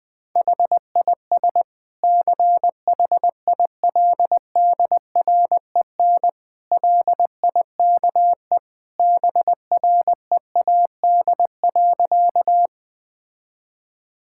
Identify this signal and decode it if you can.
Morse code, heard as a keyed tone, about 20 wpm, 700 Hz. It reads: HIS CHILDREN LIKE BREAD.
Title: Sentences from Top 1600 Words 20wpm